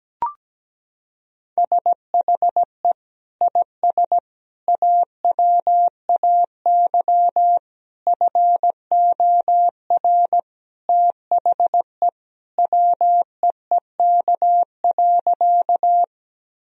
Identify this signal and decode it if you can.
Morse code, heard as a keyed tone, about 17 wpm, 700 Hz. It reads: SHE IS AWAY FOR THE WEEK.
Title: Sentences from Top 1000 Words - Review Part 2 - 17wpm